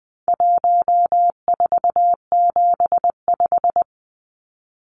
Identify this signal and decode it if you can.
Morse code, heard as a keyed tone, about 20 wpm, 700 Hz. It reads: 1475
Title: Numbers - 4-Digit 20wpm